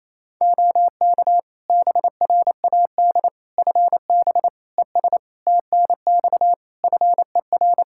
Morse code, heard as a keyed tone, about 28 wpm, 700 Hz. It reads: OK BRAD FB ES TNX FER
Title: K1USN SST QSO Practice 28wpm